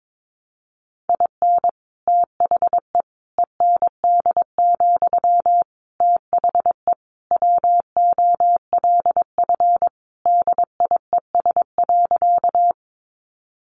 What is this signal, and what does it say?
Morse code, heard as a keyed tone, about 22 wpm, 700 Hz. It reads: IN THE END, THE WOLF DIES.
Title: Additional Easy Phrases 22wpm